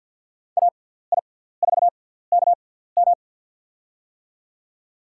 Morse code, heard as a keyed tone, about 45 wpm, spaced as for 17 wpm, 700 Hz. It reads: AI4KM
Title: USA Call Signs - 5 Characters - 6x Character Spacing - 50wpm